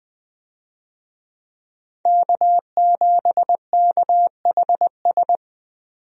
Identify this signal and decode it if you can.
Morse code, heard as a keyed tone, about 20 wpm, 700 Hz. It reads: K7KHS
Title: USA Call Signs - 5 Characters - 20wpm